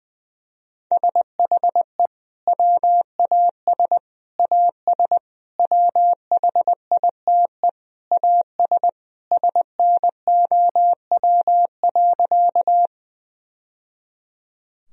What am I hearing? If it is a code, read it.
Morse code, heard as a keyed tone, about 20 wpm, 700 Hz. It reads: SHE WAS AS WHITE AS SNOW.
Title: Sentences from Top 1000 Words - Review Part 3 - 20wpm